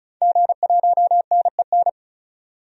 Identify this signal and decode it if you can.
Morse code, heard as a keyed tone, about 36 wpm, 700 Hz. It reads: G1NEN
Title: DXCC-100 Call Signs - 5 Characters - 35wpm